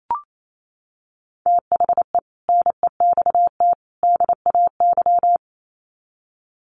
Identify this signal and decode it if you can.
Morse code, heard as a keyed tone, about 28 wpm, 700 Hz. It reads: THE NEXT DAY
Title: Sets of 3 Words 28wpm